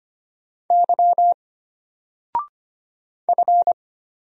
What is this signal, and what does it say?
Morse code, heard as a keyed tone, about 25 wpm, 700 Hz. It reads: Y F
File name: Single-Letter-Number-25wpm